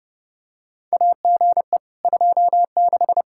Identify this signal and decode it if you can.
Morse code, heard as a keyed tone, about 31 wpm, 700 Hz. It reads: AGE 26
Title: Practice QSO - Exchange 3 - More Info - 30wpm